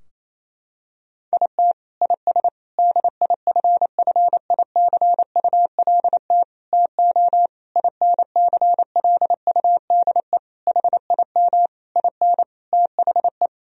Morse code, heard as a keyed tone, about 28 wpm, 700 Hz. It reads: IT IS DIFFICULT TO INCLUDE HIM IN THE
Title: Sentences from Top 1000 Words - Review Part 3 - 28wpm